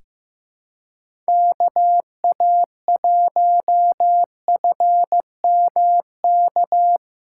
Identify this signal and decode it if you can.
Morse code, heard as a keyed tone, about 15 wpm, 700 Hz. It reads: KA1FMK
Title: USA Callsign DE Callsign - 6 Characters - 15wpm